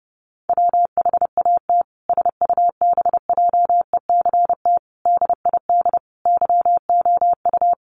Morse code, heard as a keyed tone, about 30 wpm, 700 Hz. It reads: WHAT SUBJECT DID YOU
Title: Sentences from Top 1500 Words 30wpm